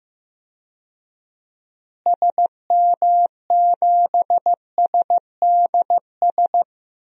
Morse code, heard as a keyed tone, about 15 wpm, 700 Hz. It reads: SM7SDS
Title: DXCC-100 Call Signs - 6 Characters - Repeated 3x - 15wpm